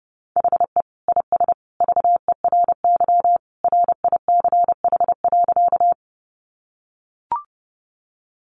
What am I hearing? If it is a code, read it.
Morse code, heard as a keyed tone, about 30 wpm, 700 Hz. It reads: HE IS VERY RICH.
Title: Sentences from Top 1100 Words 30wpm